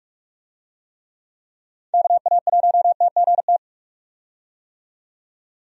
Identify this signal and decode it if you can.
Morse code, heard as a keyed tone, about 46 wpm, 700 Hz. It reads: KA1TGT
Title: USA Call Signs - 6 Characters - 45wpm